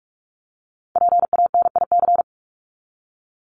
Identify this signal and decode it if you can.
Morse code, heard as a keyed tone, about 44 wpm, 700 Hz. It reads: PANIC
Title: Top 1751-2000 Words 45wpm